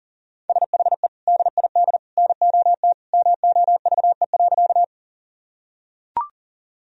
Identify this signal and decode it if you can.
Morse code, heard as a keyed tone, about 40 wpm, 700 Hz. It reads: SHE DID NOT MOVE.
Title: Sentences from Top 1000 Words - Review Part 1 - 40wpm